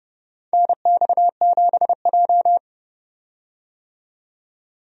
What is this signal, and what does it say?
Morse code, heard as a keyed tone, about 30 wpm, 700 Hz. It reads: NX7J
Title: USA Call Signs - 4 Characters - 30wpm